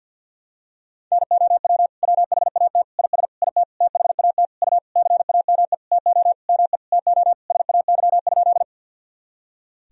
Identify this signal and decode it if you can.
Morse code, heard as a keyed tone, about 50 wpm, 700 Hz. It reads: NOW WHAT IS IT THAT U CAME TO ME TO SAY?
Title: Sentences from Top 200 Words and Abbreviations 50wpm